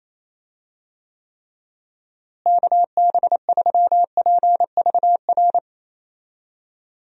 Morse code, heard as a keyed tone, about 28 wpm, 700 Hz. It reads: KB3PVR
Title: USA Call Signs - 6 Characters - Repeated 2x - 28wpm